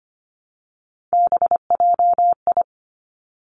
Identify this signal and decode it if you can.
Morse code, heard as a keyed tone, about 25 wpm, 700 Hz. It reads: BJI
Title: Three Challenging Letter-Number Combinations 25wpm